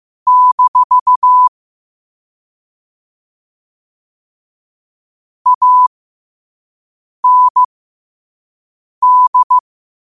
Morse code, heard as a keyed tone, about 15 wpm, spaced as for 5 wpm, 1000 Hz. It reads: - AND
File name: qotdfiles / 5wpm